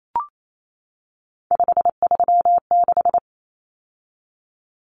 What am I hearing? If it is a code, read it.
Morse code, heard as a keyed tone, about 28 wpm, 700 Hz. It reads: 536
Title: Common Cut Numbers - 3-Digit 28wpm